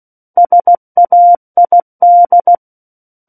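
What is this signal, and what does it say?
Morse code, heard as a keyed tone, about 16 wpm, 700 Hz. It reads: SAID
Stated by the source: Morse practice of book text at 16 wpm.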